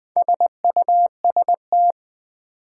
Morse code, heard as a keyed tone, about 20 wpm, 700 Hz. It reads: SUST
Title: Build A Word - 4 Letters At A Time - 11-Letter Words - 20wpm